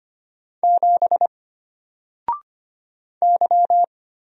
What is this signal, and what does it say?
Morse code, heard as a keyed tone, about 25 wpm, 700 Hz. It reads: 7 Y
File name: Single-Letter-Number-25wpm